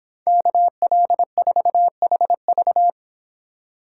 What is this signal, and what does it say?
Morse code, heard as a keyed tone, about 26 wpm, 700 Hz. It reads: KL4HV